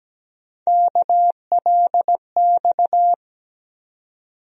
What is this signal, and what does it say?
Morse code, heard as a keyed tone, about 17 wpm, 700 Hz. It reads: KLX
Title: USA Call Sign Postfixes - 3 Letters 17wpm